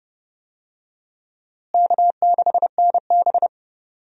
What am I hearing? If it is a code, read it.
Morse code, heard as a keyed tone, about 32 wpm, 700 Hz. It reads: K6NB
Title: USA Call Signs - 4 Characters - Repeated 2x - 30wpm